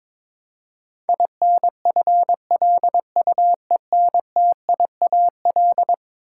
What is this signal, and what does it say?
Morse code, heard as a keyed tone, about 22 wpm, 700 Hz. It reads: INFLUENTIAL
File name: practice / 22wpm / Eleven Letter Words - 22wpm